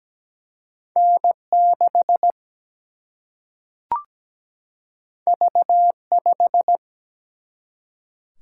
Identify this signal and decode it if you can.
Morse code, heard as a keyed tone, about 17 wpm, 700 Hz. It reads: N6 V5
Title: Single-Letter Single-Number 17wpm